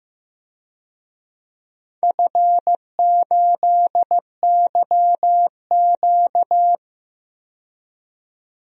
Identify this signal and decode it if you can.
Morse code, heard as a keyed tone, about 15 wpm, 700 Hz. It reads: F8YQ
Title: DXCC-100 Call Signs - 4 Characters - 15wpm